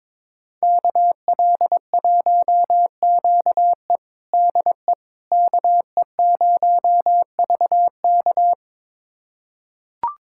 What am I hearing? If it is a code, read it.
Morse code, heard as a keyed tone, about 21 wpm, 700 Hz. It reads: KL1QE DE KE0VK
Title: USA Callsign DE Callsign - 5 Characters - 22wpm